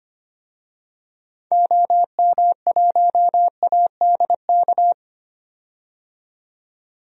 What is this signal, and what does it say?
Morse code, heard as a keyed tone, about 25 wpm, 700 Hz. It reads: OM1ADK
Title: DXCC-100 Call Signs - 6 Characters - 25wpm